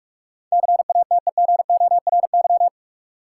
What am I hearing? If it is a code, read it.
Morse code, heard as a keyed tone, about 46 wpm, 700 Hz. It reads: CATEGORY
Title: Sentences from Top 1000 Words - Review Part 7 - New Word Drill - 45wpm